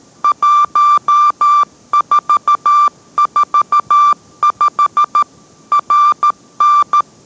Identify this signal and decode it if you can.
Morse code, heard as a keyed tone, about 13 wpm, 1180 Hz. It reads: 1445 RN